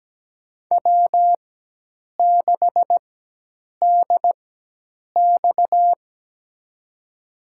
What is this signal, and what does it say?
Morse code, heard as a keyed tone, about 17 wpm, spaced as for 8 wpm, 700 Hz. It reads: W6DX
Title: USA Call Signs - 4 Characters - 4x Character Spacing - 17wpm